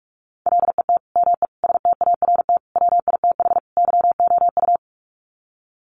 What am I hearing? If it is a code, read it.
Morse code, heard as a keyed tone, about 44 wpm, 700 Hz. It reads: LET ME START WITH YOU
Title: Sets of 5 Words Encore 45wpm